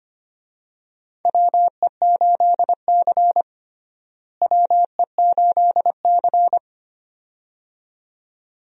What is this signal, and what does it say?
Morse code, heard as a keyed tone, about 25 wpm, 700 Hz. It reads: WE8C WE8C
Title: USA Call Signs - 4 Characters - Repeated 2x - 25wpm